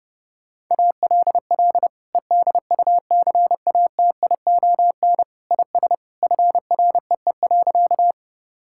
Morse code, heard as a keyed tone, about 32 wpm, 700 Hz. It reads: ALL EDUCATION IS FREE.
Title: Sentences from Top 1000 Words - Review Part 6 - 30wpm